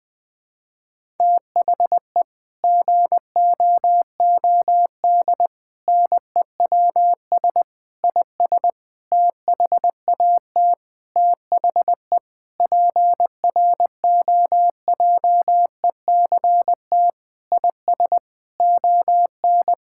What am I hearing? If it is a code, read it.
Morse code, heard as a keyed tone, about 20 wpm, 700 Hz. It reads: THE GOOD NEWS IS THAT THE PROJECT IS ON
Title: Sentences with 6-Word Phrases - 20wpm